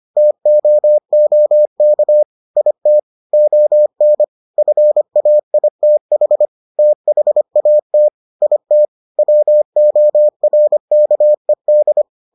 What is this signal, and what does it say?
Morse code, heard as a keyed tone, about 25 wpm, 600 Hz. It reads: TOOK IT ON FAITH THAT IT WORKED